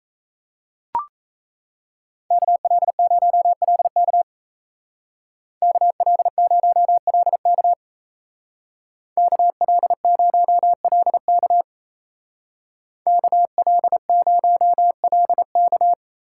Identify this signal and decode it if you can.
Morse code, heard as a keyed tone, about 37 wpm, 700 Hz. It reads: KL0LK KL0LK KL0LK KL0LK